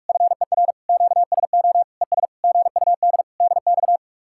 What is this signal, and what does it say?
Morse code, heard as a keyed tone, about 44 wpm, 700 Hz. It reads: FER QSO ES GUD DX